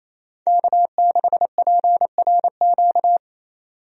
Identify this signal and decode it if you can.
Morse code, heard as a keyed tone, about 27 wpm, 700 Hz. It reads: K6PRQ
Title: USA Call Signs - 5 Characters - 28wpm